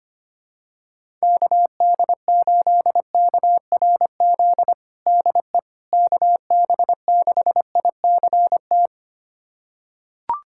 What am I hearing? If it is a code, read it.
Morse code, heard as a keyed tone, about 25 wpm, 700 Hz. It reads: KD8KRZ DE KB6ICT
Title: USA Callsign DE Callsign 25wpm